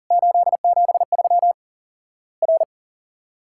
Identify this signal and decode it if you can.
Morse code, heard as a keyed tone, about 39 wpm, 700 Hz. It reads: 873 R
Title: Worked All Europe DX Contest - QTC Exchange - 40wpm